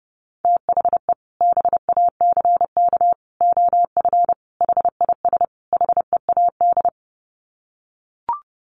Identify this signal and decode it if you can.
Morse code, heard as a keyed tone, about 30 wpm, 700 Hz. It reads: THE BACK OF HIS HEAD
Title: Sets of 5 Words Encore 30wpm